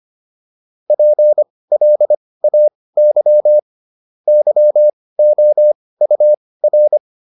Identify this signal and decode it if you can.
Morse code, heard as a keyed tone, about 25 wpm, spaced as for 18 wpm, 600 Hz. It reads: PLAY YOUR